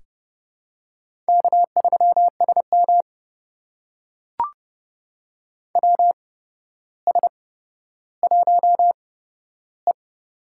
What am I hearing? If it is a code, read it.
Morse code, heard as a keyed tone, about 30 wpm, spaced as for 13 wpm, 700 Hz. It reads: K3SM WS1E
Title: USA Call Signs - 4 Characters - 8x Character Spacing - 30wpm